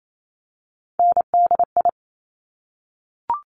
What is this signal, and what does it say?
Morse code, heard as a keyed tone, about 28 wpm, 700 Hz. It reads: NDI
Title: Top Three Letter Combinations in Words 28wpm